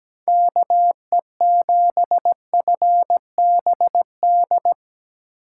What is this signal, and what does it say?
Morse code, heard as a keyed tone, about 19 wpm, 700 Hz. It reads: KE7FBD
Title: Speed Racing - USA 2x3 Call Signs 17wpm